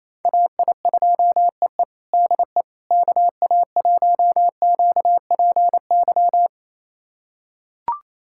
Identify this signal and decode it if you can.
Morse code, heard as a keyed tone, about 28 wpm, 700 Hz. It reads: AI2EE DE KA1QPY
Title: USA Callsign DE Callsign 28wpm